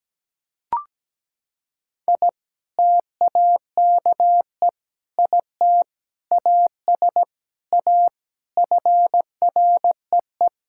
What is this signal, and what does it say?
Morse code, heard as a keyed tone, about 17 wpm, 700 Hz. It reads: I TAKE IT AS A FREE
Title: Sentences from Top 1000 Words - Review Part 6 - 17wpm